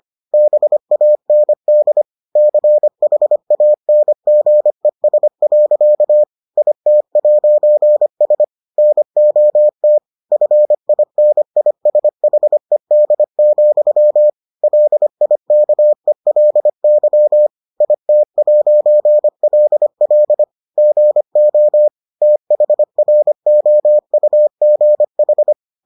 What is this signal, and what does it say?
Morse code, heard as a keyed tone, about 25 wpm, 600 Hz. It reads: BAND CHANGES. IT'S NOT FINISHED, LIKELY IT'LL GO THROUGH